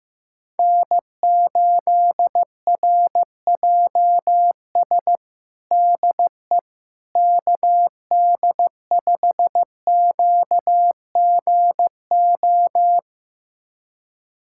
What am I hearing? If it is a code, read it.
Morse code, heard as a keyed tone, about 15 wpm, 700 Hz. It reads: N8RJS DE KD5QGO
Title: USA Callsign DE Callsign 15wpm